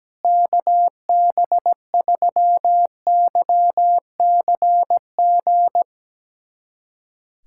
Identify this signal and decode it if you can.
Morse code, heard as a keyed tone, about 17 wpm, 700 Hz. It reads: KB3YCG
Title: USA Callsign DE Callsign - 6 Characters - 17wpm